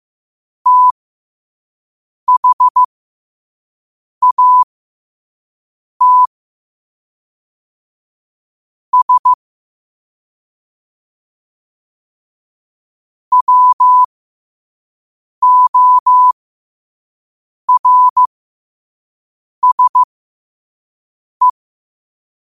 Morse code, heard as a keyed tone, about 15 wpm, spaced as for 5 wpm, 1000 Hz. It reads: THAT S WORSE
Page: Quotes for Fri, 15 Aug 2025 in Morse Code at 5 words per minute.